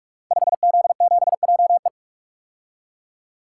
Morse code, heard as a keyed tone, about 44 wpm, 700 Hz. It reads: 5Z7JE
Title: Code Groups - 5 Letter:Numbers - 45wpm